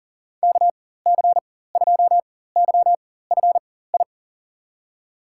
Morse code, heard as a keyed tone, about 40 wpm, spaced as for 19 wpm, 700 Hz. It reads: KC2YFI
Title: USA Call Signs - 4x Character Spacing - 40wpm